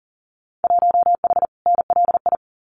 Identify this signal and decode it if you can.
Morse code, heard as a keyed tone, about 40 wpm, 700 Hz. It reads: 1H NLI
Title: Winter Field Day Exchanges 40wpm